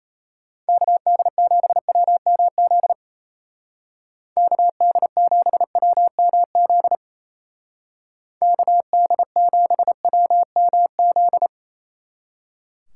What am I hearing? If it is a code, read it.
Morse code, heard as a keyed tone, about 35 wpm, 700 Hz. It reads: KD7WMZ KD7WMZ KD7WMZ
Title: Speed Racing - USA 2x3 Call Signs 28wpm